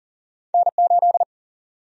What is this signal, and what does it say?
Morse code, heard as a keyed tone, about 40 wpm, 700 Hz. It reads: N8
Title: USA Call Sign Prefixes - 40wpm